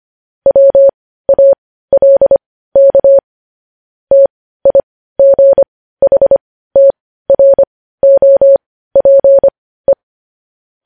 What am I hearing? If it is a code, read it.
Morse code, heard as a keyed tone, about 25 wpm, spaced as for 15 wpm, 550 Hz. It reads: WALK TIGHTROPE
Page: News Headlines in Morse Code at 15 WPM